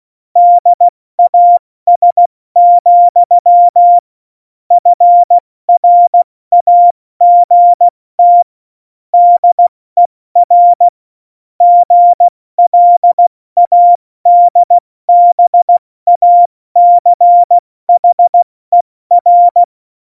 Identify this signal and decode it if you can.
Morse code, heard as a keyed tone, about 16 wpm, spaced as for 14 wpm, 700 Hz. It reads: DAS, FRAGT DER GLADBACHER
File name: Morsekurs / Witze / Witz 22. Dezember 2020, 70 F80